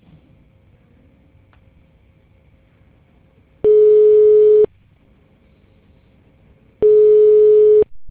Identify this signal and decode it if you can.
Ringback tone (Europe, 425 Hz).